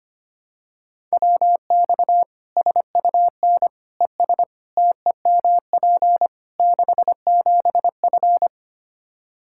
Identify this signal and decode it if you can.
Morse code, heard as a keyed tone, about 25 wpm, 700 Hz. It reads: WX SUN ES TEMP 67F
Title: Practice QSO - Exchange 2 - Info - 25wpm